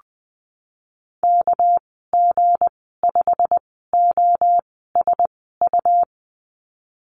Morse code, heard as a keyed tone, about 20 wpm, spaced as for 15 wpm, 700 Hz. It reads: KG5OSU
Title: 207 Lesson 33g - Letter j - Call Signs 20wpm